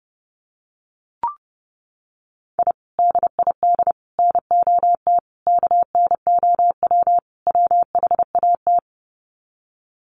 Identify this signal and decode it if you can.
Morse code, heard as a keyed tone, about 30 wpm, 700 Hz. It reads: I DID NOT KNOW WHAT
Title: Sets of 5 Words 30wpm